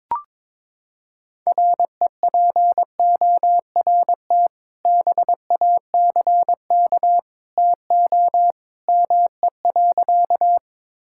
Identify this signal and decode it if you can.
Morse code, heard as a keyed tone, about 22 wpm, 700 Hz. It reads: REPORT BACK TO ME.
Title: Sentences from Top 1000 Words - Review Part 6 - 22wpm